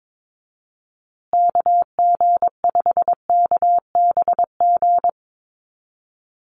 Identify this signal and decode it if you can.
Morse code, heard as a keyed tone, about 22 wpm, 690 Hz. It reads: KG5KBG
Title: USA Call Signs Repeated 3x 22wpm